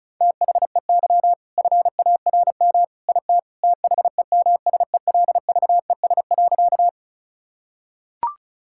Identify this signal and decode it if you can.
Morse code, heard as a keyed tone, about 36 wpm, 700 Hz. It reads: THEY FARM IT THEMSELVES.
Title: Sentences from Top 1000 Words - Review Part 5 - 35wpm